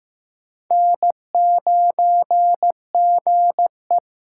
Morse code, heard as a keyed tone, about 15 wpm, 700 Hz. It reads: N9GE
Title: USA Call Signs - 4 Characters - 4x Character Spacing - 15wpm